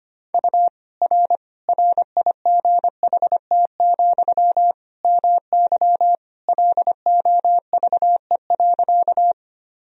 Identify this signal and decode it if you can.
Morse code, heard as a keyed tone, about 25 wpm, 700 Hz. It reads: U R RIGHT, MY LOVE.
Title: Sentences from Top 400 Words and Abbreviations 25wpm